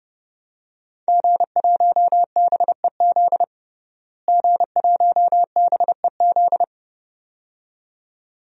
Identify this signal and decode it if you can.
Morse code, heard as a keyed tone, about 32 wpm, 700 Hz. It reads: G1BEZ G1BEZ
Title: DXCC-100 Call Signs - 5 Characters - Repeated 2x - 30wpm